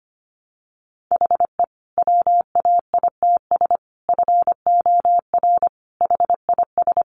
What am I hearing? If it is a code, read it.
Morse code, heard as a keyed tone, about 25 wpm, 700 Hz. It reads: HE WAITS FOR HIS
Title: Additional Easy Phrases 25wpm